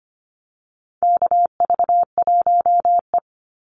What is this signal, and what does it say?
Morse code, heard as a keyed tone, about 25 wpm, 700 Hz. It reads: KV1E
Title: Call Signs - 25wpm